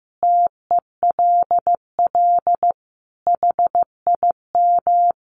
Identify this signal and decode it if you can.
Morse code, heard as a keyed tone, about 15 wpm, 700 Hz. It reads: TELL HIM
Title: Sentences from Top 700 Words 15wpm